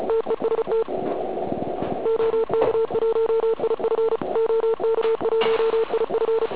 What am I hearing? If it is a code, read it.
Morse code, heard as a keyed tone, about 34 wpm, 450 Hz, with noise. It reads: TEST OK1IF OK1IF
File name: Archiv / CUC106 / cuc106 PILEUP